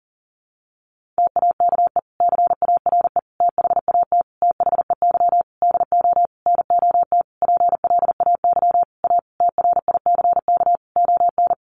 Morse code, heard as a keyed tone, about 40 wpm, 700 Hz. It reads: TAKE CARE THAT THEY DO NOT PLAY A TRICK ON
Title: Sentences from Top 1500 Words 40wpm